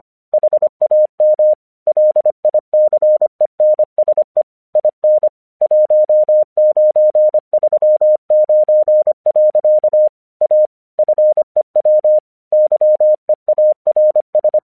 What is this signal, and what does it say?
Morse code, heard as a keyed tone, about 24 wpm, 610 Hz. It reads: HAM LICENSE IN 1939. A FEW YEARS
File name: 1 English 25wpm / AQC09